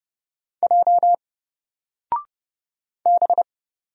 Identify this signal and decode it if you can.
Morse code, heard as a keyed tone, about 30 wpm, 700 Hz. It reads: J B
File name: Single-Letter-Number-30wpm